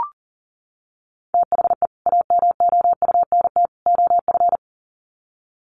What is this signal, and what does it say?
Morse code, heard as a keyed tone, about 40 wpm, 700 Hz. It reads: THE AMOUNT OF
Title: Sets of 3 Words 40wpm